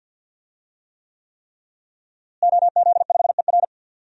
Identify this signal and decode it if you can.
Morse code, heard as a keyed tone, about 52 wpm, 700 Hz. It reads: OZ5ER
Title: DXCC-100 Call Signs - 5 Characters - Repeated 3x - 50wpm